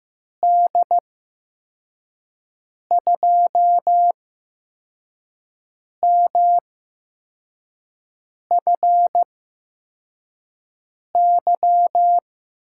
Morse code, heard as a keyed tone, about 15 wpm, spaced as for 4 wpm, 700 Hz. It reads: D2MFY
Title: USA Call Signs - 6 Characters - 8x Character Spacing - 15wpm